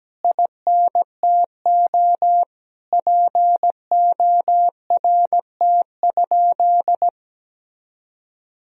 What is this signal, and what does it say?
Morse code, heard as a keyed tone, about 17 wpm, 700 Hz. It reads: INTO PORT?
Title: Sentences from Top 1000 Words - Review Part 1 - 17wpm